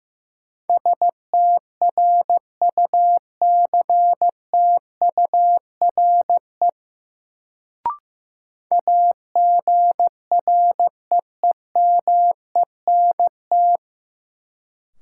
Morse code, heard as a keyed tone, about 15 wpm, 700 Hz. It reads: STRUCTURE AGREEMENT
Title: Sentences from Top 1500 Words - New Word Drill Alt - 15wpm